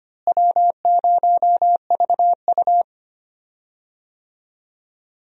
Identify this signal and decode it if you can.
Morse code, heard as a keyed tone, about 25 wpm, 700 Hz. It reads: W0VU
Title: USA Call Signs - 4 Characters - Repeated 2x - 25wpm